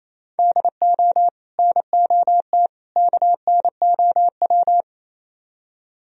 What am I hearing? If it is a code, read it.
Morse code, heard as a keyed tone, about 35 wpm, 700 Hz. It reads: DO NOT KNOW